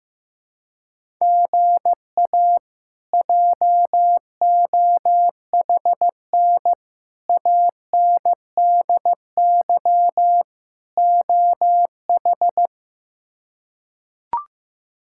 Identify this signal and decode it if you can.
Morse code, heard as a keyed tone, about 15 wpm, 700 Hz. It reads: GA JOHN ANDY OH
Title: K1USN SST QSO Practice - 15wpm